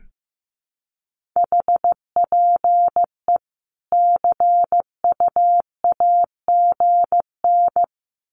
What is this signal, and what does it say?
Morse code, heard as a keyed tone, about 15 wpm, 700 Hz. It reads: HPE CUAGN
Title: Common QSO Phrases 15wpm